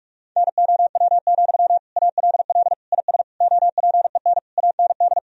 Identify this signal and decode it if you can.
Morse code, heard as a keyed tone, about 46 wpm, 700 Hz. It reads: NOW, ALL IS OPEN AND
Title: Sentences from Top 1000 Words - Review Part 4 - 45wpm